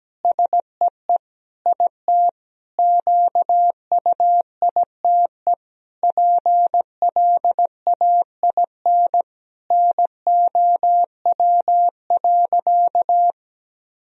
Morse code, heard as a keyed tone, about 17 wpm, 700 Hz. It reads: SEE IT QUITE PLAIN NOW.
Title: Sentences from Top 1000 Words - Review Part 3 - 17wpm